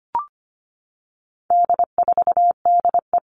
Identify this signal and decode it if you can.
Morse code, heard as a keyed tone, about 25 wpm, 700 Hz. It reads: D4DE
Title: All Cut Numbers - 4-Digit 25wpm